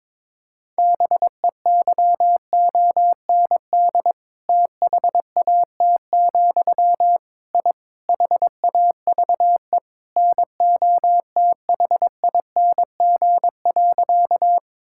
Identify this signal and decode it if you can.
Morse code, heard as a keyed tone, about 22 wpm, 700 Hz. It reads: BEYOND THAT, I HAVE NOTHING.
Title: Sentences from Top 1000 Words - Review Part 6 - 22wpm